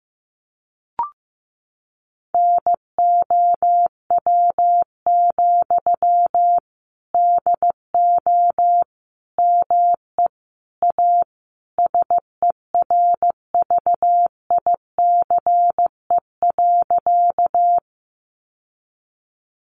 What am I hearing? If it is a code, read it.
Morse code, heard as a keyed tone, about 14 wpm, 700 Hz. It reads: NOW, DO ME A SERVICE.
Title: Sentences from Top 700 Words 15wpm